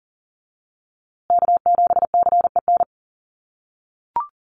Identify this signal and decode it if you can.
Morse code, heard as a keyed tone, about 40 wpm, 700 Hz. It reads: K7CEN
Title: USA Call Signs Repeated 3x 40wpm